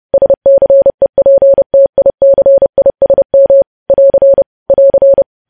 Morse code, heard as a keyed tone, about 30 wpm, 550 Hz. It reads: SCEPTICISM <AR> <AR>
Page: news headlines in morse code at 30 wpm